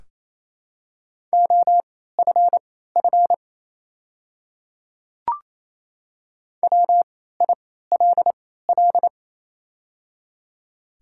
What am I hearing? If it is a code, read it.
Morse code, heard as a keyed tone, about 27 wpm, spaced as for 16 wpm, 690 Hz. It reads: OFF WILL
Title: 3x Character Spacing - Top 300 Words - 28wpm